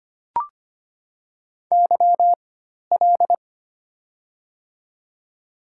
Morse code, heard as a keyed tone, about 25 wpm, spaced as for 12 wpm, 700 Hz. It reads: YL
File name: Top 90 QSO Elements - 4x Character Spacing - 25wpm